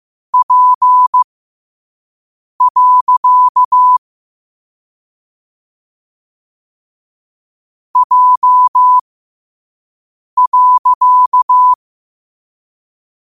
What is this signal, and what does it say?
Morse code, heard as a keyed tone, about 15 wpm, spaced as for 5 wpm, 1000 Hz. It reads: P. J.
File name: qotdfiles / 5wpm